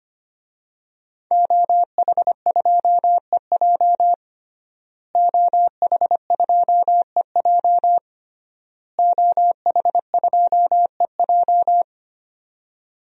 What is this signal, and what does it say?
Morse code, heard as a keyed tone, about 25 wpm, 700 Hz. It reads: OH2EJ OH2EJ OH2EJ
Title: DXCC-100 Call Signs - 5 Characters - Repeated 3x - 25wpm